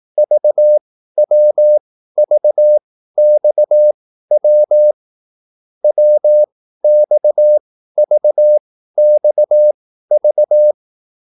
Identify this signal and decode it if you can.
Morse code, heard as a keyed tone, about 18 wpm, spaced as for 13 wpm, 600 Hz. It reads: VWVXW WXVXV